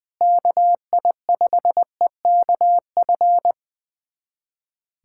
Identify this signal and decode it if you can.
Morse code, heard as a keyed tone, about 20 wpm, 700 Hz. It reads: KI5EKF
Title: USA Callsign DE Callsign 20wpm